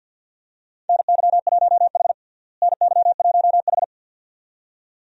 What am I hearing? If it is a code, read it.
Morse code, heard as a keyed tone, about 46 wpm, 700 Hz. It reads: NY1H NY1H
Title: USA Call Signs - 4 Characters - Repeated 2x - 50wpm